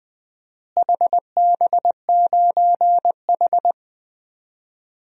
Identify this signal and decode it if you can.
Morse code, heard as a keyed tone, about 20 wpm, 700 Hz. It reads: HB9H
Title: DXCC-100 Call Signs - 4 Characters - Repeated 3x - 20wpm